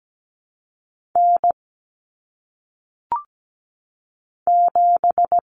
Morse code, heard as a keyed tone, about 17 wpm, 700 Hz. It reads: N 7
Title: Common Cut Numbers - 1-Digit 17wpm